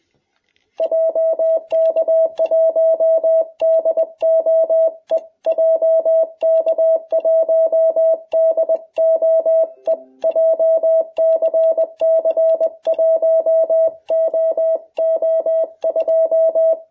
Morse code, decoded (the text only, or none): JK1DOE JK1DOE JCC 1OO2